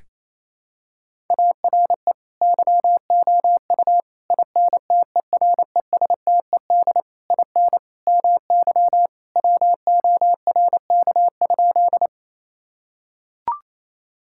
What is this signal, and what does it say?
Morse code, heard as a keyed tone, about 28 wpm, 700 Hz. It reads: ARE YOU INTERESTED IN MY WORK?
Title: Sentences from Top 1000 Words - Review Part 6 - 28wpm